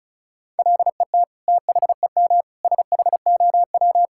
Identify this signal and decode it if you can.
Morse code, heard as a keyed tone, about 36 wpm, 690 Hz. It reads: LET THEM SHOW
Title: Sentences from Top 1000 Words - Review Part 5 - 35wpm